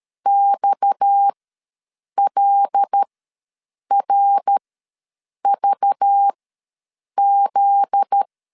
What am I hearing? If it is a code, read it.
Morse code, heard as a keyed tone, about 13 wpm, spaced as for 7 wpm, 800 Hz. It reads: XLRVZ
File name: LEZIONI / 9 28 ws 12wpm 3354